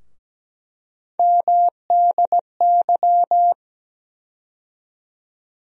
Morse code, heard as a keyed tone, about 17 wpm, 700 Hz. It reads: MDY
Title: USA Call Sign Postfixes - 3 Letters 17wpm